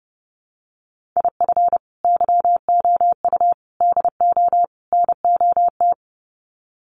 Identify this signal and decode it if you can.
Morse code, heard as a keyed tone, about 30 wpm, 700 Hz. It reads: IF YOU DO NOT
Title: Sets of 4 Words 30wpm